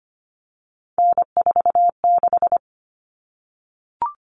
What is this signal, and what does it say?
Morse code, heard as a keyed tone, about 25 wpm, 700 Hz. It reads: N46
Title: Common Cut Numbers - 3-Digit 25wpm